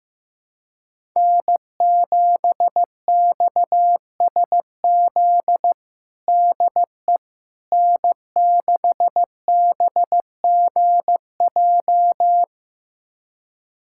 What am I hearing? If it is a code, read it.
Morse code, heard as a keyed tone, about 15 wpm, 700 Hz. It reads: N7XSZ DE N6BGJ
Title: USA Callsign DE Callsign - 5 Characters - 15wpm